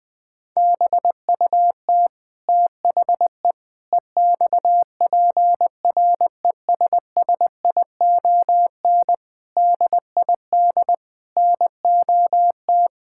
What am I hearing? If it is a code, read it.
Morse code, heard as a keyed tone, about 20 wpm, 700 Hz. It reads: BUT THE EXPRESSION DID NOT
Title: Sentences from Top 1600 Words 20wpm